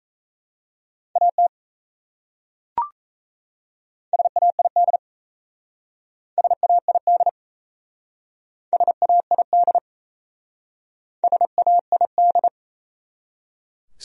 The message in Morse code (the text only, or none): AT SAID SAID SAID SAID